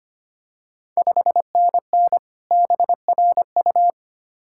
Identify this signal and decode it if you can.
Morse code, heard as a keyed tone, about 25 wpm, 690 Hz. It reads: 5NN BRU
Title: Ontario QSO Party Exchange 25wpm